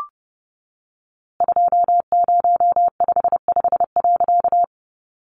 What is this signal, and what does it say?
Morse code, heard as a keyed tone, about 30 wpm, 700 Hz. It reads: 2055.
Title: A Sound Of Thunder 30wpm